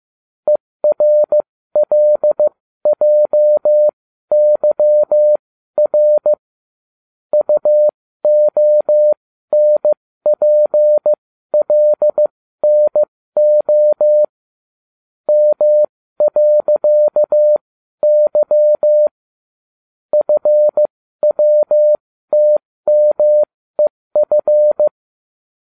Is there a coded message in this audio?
Yes: Morse code from 0.5 to 24.9 s, heard as a keyed tone, about 15 wpm, spaced as for 13 wpm, 600 Hz. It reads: ERLJYR UONPLNO M.Y FWTMEF